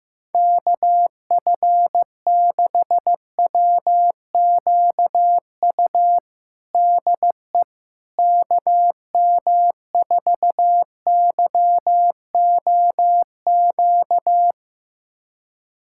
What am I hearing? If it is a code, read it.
Morse code, heard as a keyed tone, about 15 wpm, 700 Hz. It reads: KF6WQU DE KM4YOQ
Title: USA Callsign DE Callsign - 6 Characters - 15wpm